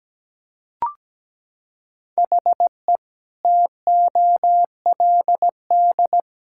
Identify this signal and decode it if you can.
Morse code, heard as a keyed tone, about 17 wpm, 700 Hz. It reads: HE TOLD
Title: Sentences from Top 1600 Words 17wpm